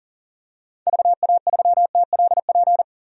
Morse code, heard as a keyed tone, about 39 wpm, 700 Hz. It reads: VA3TLP
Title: DXCC-100 Call Signs - 6 Characters - 40wpm